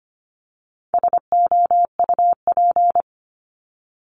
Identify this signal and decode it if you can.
Morse code, heard as a keyed tone, about 25 wpm, 700 Hz. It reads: SOUP
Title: Top 751-1000 Words 25wpm